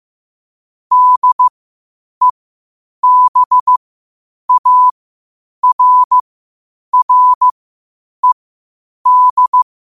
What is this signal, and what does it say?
Morse code, heard as a keyed tone, about 15 wpm, spaced as for 8 wpm, 1000 Hz. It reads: DEBARRED
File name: qotdfiles / 8wpm